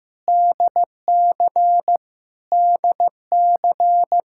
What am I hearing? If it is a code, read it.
Morse code, heard as a keyed tone, about 15 wpm, 700 Hz. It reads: DC DC
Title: Practice QSO - Exchange 1 - Beginning - 15wpm